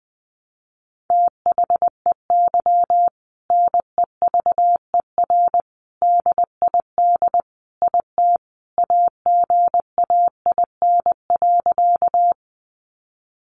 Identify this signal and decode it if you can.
Morse code, heard as a keyed tone, about 20 wpm, 700 Hz. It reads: THEY NEVER DID IT AGAIN.
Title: Sentences from Top 300 Words 20wpm